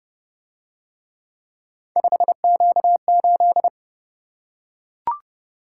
Morse code, heard as a keyed tone, about 30 wpm, 700 Hz. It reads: 5Q8